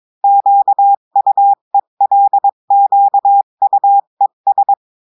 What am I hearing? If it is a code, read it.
Morse code, heard as a keyed tone, about 22 wpm, 800 Hz. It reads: QUELQUES
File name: cours-du-mercredi / audios / cours-853-du-2-mai-2024-etxte chiffre et texte clair-a-20-wpm